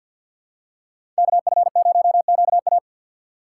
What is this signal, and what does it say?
Morse code, heard as a keyed tone, about 50 wpm, 700 Hz. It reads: KF0QA
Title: USA Call Signs - 5 Characters - 4x Character Spacing - 50wpm